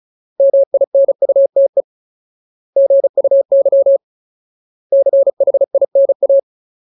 Morse code, heard as a keyed tone, about 34 wpm, 550 Hz. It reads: MINUTE GUY CHINA
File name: Ultimate Word List / 35WPM Ultimate